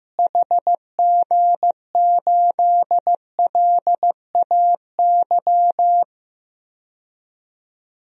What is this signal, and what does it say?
Morse code, heard as a keyed tone, about 16 wpm, 700 Hz. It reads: HG8LAY
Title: DXCC-100 Call Signs - 6 Characters - Repeated 2x - 15wpm